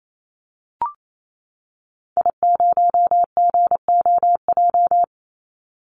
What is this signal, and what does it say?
Morse code, heard as a keyed tone, about 28 wpm, 700 Hz. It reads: I0GOJ
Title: Call Signs - 28wpm